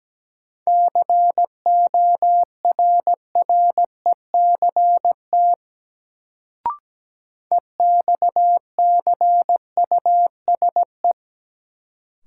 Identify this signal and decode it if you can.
Morse code, heard as a keyed tone, about 17 wpm, 700 Hz. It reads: CORRECT EXCUSE
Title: Sentences from Top 700 Words - New Word Drill - 17wpm